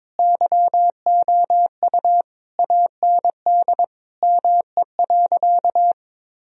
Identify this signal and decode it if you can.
Morse code, heard as a keyed tone, about 22 wpm, 700 Hz. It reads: YOU AND ME.
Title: Sentences from Top 1000 Words - Review Part 1 - 22wpm